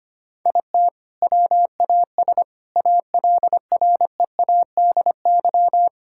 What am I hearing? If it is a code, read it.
Morse code, heard as a keyed tone, about 25 wpm, 700 Hz. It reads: IT WAS ALREADY
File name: practice / 25wpm / All-Age Appropriate Jokes - 25wpm